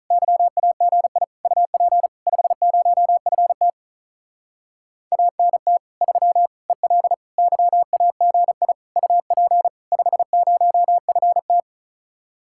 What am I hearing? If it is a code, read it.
Morse code, heard as a keyed tone, about 44 wpm, 700 Hz. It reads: YAGI UP 50FT ANT 3 EL YAGI UP 50FT